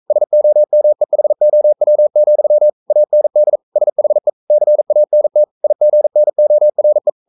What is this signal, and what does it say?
Morse code, heard as a keyed tone, about 42 wpm, 600 Hz. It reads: SOMEHOW, AND SHE CANT IGNORE
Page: Romanen Never af Ken Follet på engelsk.